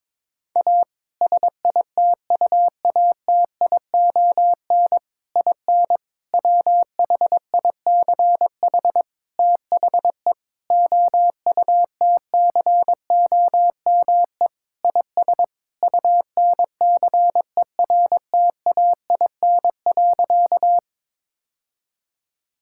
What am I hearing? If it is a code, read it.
Morse code, heard as a keyed tone, about 22 wpm, 700 Hz. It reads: A SITUATION IN WHICH THE OUTCOME IS UNCERTAIN.
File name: practice / 22wpm / Sentences with 3-Word Phrases - 22wpm